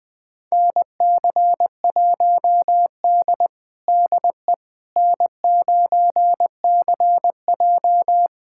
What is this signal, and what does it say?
Morse code, heard as a keyed tone, about 21 wpm, 700 Hz. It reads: NC1D DE N9CJ
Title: USA Callsign DE Callsign - 4 Characters - 20wpm